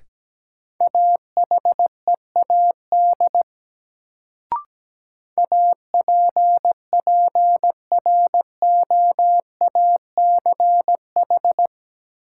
Morse code, heard as a keyed tone, about 17 wpm, 700 Hz. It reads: AHEAD APPROACH
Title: Sentences from Top 1000 Words - Review Part 6 - New Word Drill - 17wpm